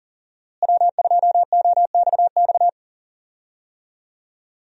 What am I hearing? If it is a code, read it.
Morse code, heard as a keyed tone, about 40 wpm, 700 Hz. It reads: W2OXX
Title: USA Call Signs - 5 Characters - Repeated 2x - 40wpm